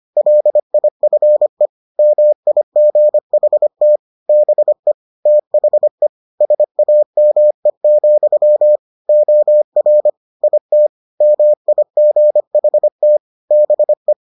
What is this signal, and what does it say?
Morse code, heard as a keyed tone, about 25 wpm, 600 Hz. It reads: LIFE MIGHT BE THE SAME, OR IT MIGHT BE